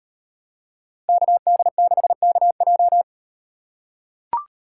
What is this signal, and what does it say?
Morse code, heard as a keyed tone, about 38 wpm, 700 Hz. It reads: KD6KJ